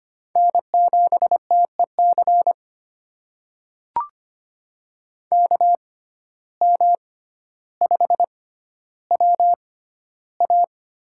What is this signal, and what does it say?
Morse code, heard as a keyed tone, about 25 wpm, spaced as for 13 wpm, 700 Hz. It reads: N7TEC KM5WA
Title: USA Call Signs - 5 Characters - 6x Character Spacing - 25wpm